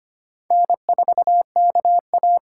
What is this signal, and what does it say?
Morse code, heard as a keyed tone, about 31 wpm, 700 Hz. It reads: N4KA
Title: Speed Racing - USA Call Signs 25wpm